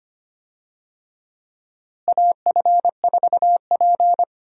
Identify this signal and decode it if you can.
Morse code, heard as a keyed tone, about 25 wpm, 700 Hz. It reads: AF4P
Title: USA Call Signs - 4 Characters - Repeated 3x - 25wpm